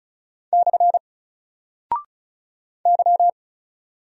Morse code, heard as a keyed tone, about 36 wpm, 700 Hz. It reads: / Y
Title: Single Character (Letter:Number:Punctuation:Prosign) Alt 35wpm